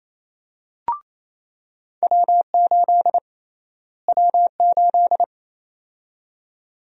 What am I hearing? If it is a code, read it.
Morse code, heard as a keyed tone, about 28 wpm, 700 Hz. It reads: W8 W8
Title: USA Call Sign Prefixes - Repeated 2x - 28wpm